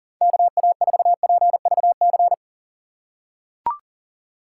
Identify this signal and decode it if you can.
Morse code, heard as a keyed tone, about 40 wpm, 700 Hz. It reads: KA4PVC
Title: USA Call Signs - 6 Characters - 4x Character Spacing - 40wpm